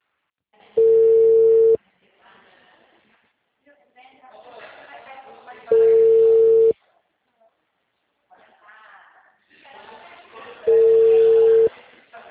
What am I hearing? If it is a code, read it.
Ringback tone (China, 450 Hz), with speech.
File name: out- -2003- - - .122